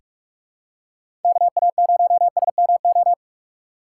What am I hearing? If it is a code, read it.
Morse code, heard as a keyed tone, about 46 wpm, 700 Hz. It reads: KA0SMO
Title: USA Call Signs - 6 Characters - 6x Character Spacing - 45wpm